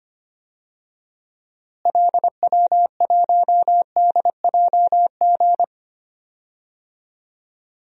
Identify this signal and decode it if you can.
Morse code, heard as a keyed tone, about 25 wpm, 700 Hz. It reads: LW1DJG
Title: DXCC-100 Call Signs - 6 Characters - Repeated 2x - 25wpm